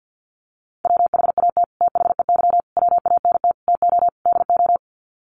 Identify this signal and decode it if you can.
Morse code, heard as a keyed tone, about 50 wpm, 700 Hz. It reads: WHAT THEY WANT TO DO
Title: Sets of 5 Words Encore2 50wpm